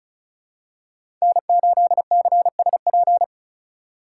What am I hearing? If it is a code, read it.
Morse code, heard as a keyed tone, about 34 wpm, 700 Hz. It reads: N8CSP
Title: USA Call Signs - 5 Characters - 8x Character Spacing - 35wpm